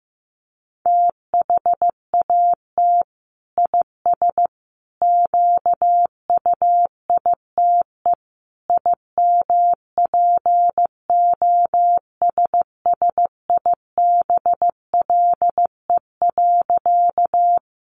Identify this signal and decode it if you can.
Morse code, heard as a keyed tone, about 14 wpm, 700 Hz. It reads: THAT IS QUITE IMPOSSIBLE.
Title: Sentences from Top 1100 Words 15wpm